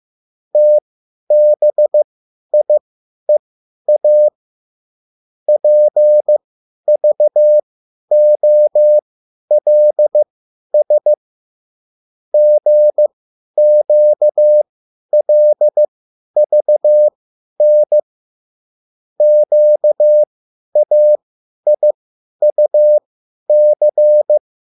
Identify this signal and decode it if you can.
Morse code, heard as a keyed tone, about 15 wpm, spaced as for 10 wpm, 600 Hz. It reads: TBIEA PVOLS GQLVN QAIUC